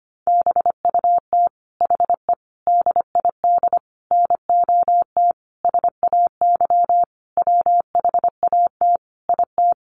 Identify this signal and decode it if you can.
Morse code, heard as a keyed tone, about 25 wpm, 700 Hz. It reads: BUT HE DID NOT SAY WHAT IT
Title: Sentences from Top 600 Words 25wpm